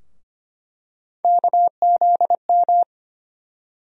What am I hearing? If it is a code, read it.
Morse code, heard as a keyed tone, about 25 wpm, 700 Hz. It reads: KZM
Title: USA Call Sign Postfixes - 3 Letters 25wpm